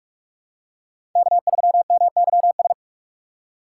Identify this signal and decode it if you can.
Morse code, heard as a keyed tone, about 46 wpm, 700 Hz. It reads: K3MYS
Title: USA Call Signs - 5 Characters - 2x Character Spacing - 45wpm